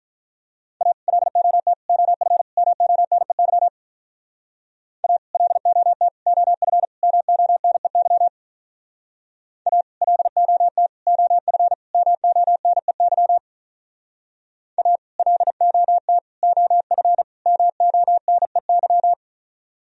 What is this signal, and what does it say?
Morse code, heard as a keyed tone, about 45 wpm, 700 Hz. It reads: A LOT OF MONEY A LOT OF MONEY A LOT OF MONEY A LOT OF MONEY